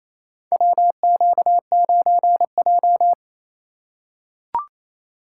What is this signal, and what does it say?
Morse code, heard as a keyed tone, about 28 wpm, 700 Hz. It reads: WQ9J
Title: USA Call Signs - 4 Characters - 4x Character Spacing - 28wpm